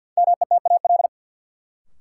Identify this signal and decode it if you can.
Morse code, heard as a keyed tone, about 47 wpm, 700 Hz. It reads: METAL
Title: Sentences from Top 1000 Words - Review Part 6 - New Word Drill - 50wpm